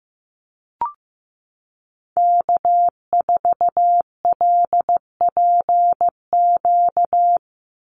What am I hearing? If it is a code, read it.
Morse code, heard as a keyed tone, about 15 wpm, 700 Hz. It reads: K4LPQ
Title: NA SOTA Activator Callsigns 15wpm